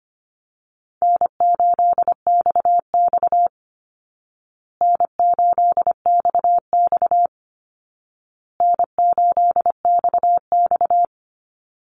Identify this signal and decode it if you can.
Morse code, heard as a keyed tone, about 25 wpm, 700 Hz. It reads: N8XX N8XX N8XX
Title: Call Signs Repeated 2x - 25wpm